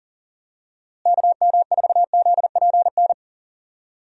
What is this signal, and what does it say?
Morse code, heard as a keyed tone, about 40 wpm, 700 Hz. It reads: KM4ZPN
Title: USA Call Signs - 6 Characters - Repeated 2x - 40wpm